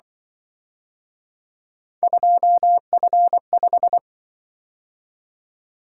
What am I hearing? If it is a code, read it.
Morse code, heard as a keyed tone, about 25 wpm, 700 Hz. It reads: 2F5
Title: Speed-Racing - Three Letter-Number Combinations - 20wpm